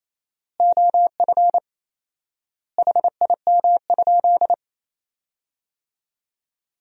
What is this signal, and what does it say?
Morse code, heard as a keyed tone, about 28 wpm, 700 Hz. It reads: OF HIM?
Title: 4x Word-Spacing - Sentences from Top 300 Words - 28wpm